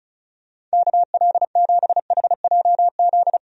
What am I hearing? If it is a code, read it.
Morse code, heard as a keyed tone, about 34 wpm, 700 Hz. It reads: KL7HJZ
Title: DXCC-100 Call Signs - 6 Characters - Repeated 2x - 35wpm